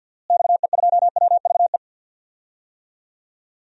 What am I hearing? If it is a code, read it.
Morse code, heard as a keyed tone, about 52 wpm, 700 Hz. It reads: XE2WVE
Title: DXCC-100 Call Signs - 6 Characters - Repeated 2x - 50wpm